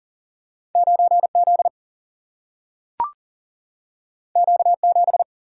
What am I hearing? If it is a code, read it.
Morse code, heard as a keyed tone, about 40 wpm, 700 Hz. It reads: 9Z Q7
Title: Single-Letter Single-Number 40wpm